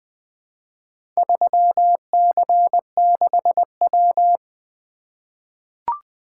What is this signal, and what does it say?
Morse code, heard as a keyed tone, about 20 wpm, 700 Hz. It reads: 3C6W
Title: Four Letter-Number Combinations - 20wpm